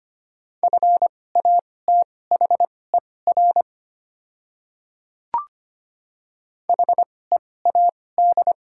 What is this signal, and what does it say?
Morse code, heard as a keyed tone, about 25 wpm, spaced as for 18 wpm, 700 Hz. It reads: FATHER HEAD
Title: 2x Character Spacing - Top 500 Words - 25wpm